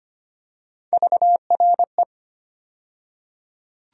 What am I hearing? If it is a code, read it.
Morse code, heard as a keyed tone, about 25 wpm, 700 Hz. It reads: VRE
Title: USA Call Sign Postfixes - 3 Letters 25wpm